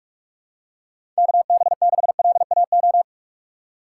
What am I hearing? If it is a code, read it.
Morse code, heard as a keyed tone, about 45 wpm, 700 Hz. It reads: KB6LAO
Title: USA Call Signs - 6 Characters - 2x Character Spacing - 45wpm